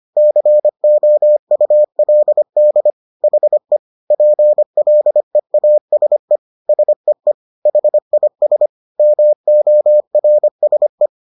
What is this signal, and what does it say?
Morse code, heard as a keyed tone, about 25 wpm, 600 Hz. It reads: COULD HE PLEASE SEE HIS MORSE